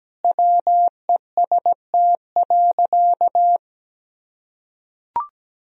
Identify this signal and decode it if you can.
Morse code, heard as a keyed tone, about 17 wpm, 700 Hz. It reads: WEST.
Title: Sentences from Top 1000 Words - Review Part 3 - 17wpm